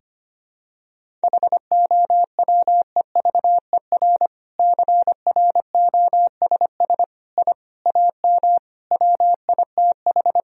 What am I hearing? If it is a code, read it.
Morse code, heard as a keyed tone, about 25 wpm, 700 Hz. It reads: HOWEVER CROSS I AM WITH
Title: Sentences from Top 1000 Words - Review Part 1 - 25wpm